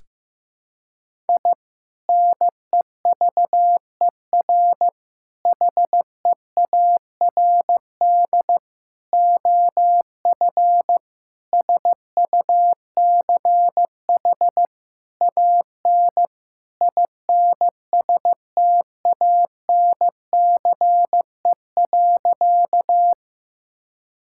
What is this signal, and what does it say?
Morse code, heard as a keyed tone, about 15 wpm, 700 Hz. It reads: I NEVER HEARD OF SUCH AN INSTANCE.
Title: Sentences from Top 1000 Words - Review Part 5 - 15wpm